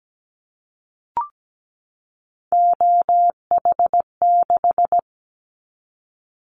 Morse code, heard as a keyed tone, about 17 wpm, 700 Hz. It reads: OH6
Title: Callsign Prefixes for Top 25 DXCC - 17wpm